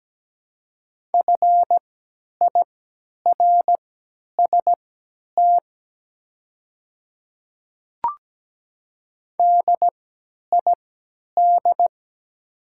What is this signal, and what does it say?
Morse code, heard as a keyed tone, about 17 wpm, spaced as for 10 wpm, 700 Hz. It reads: FIRST DID
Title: 3x Character Spacing - Top 100 Words - 17wpm